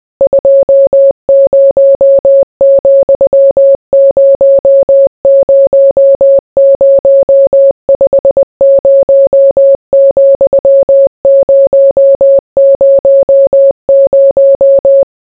Morse code, heard as a keyed tone, about 20 wpm, 550 Hz. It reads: 20,000 50,000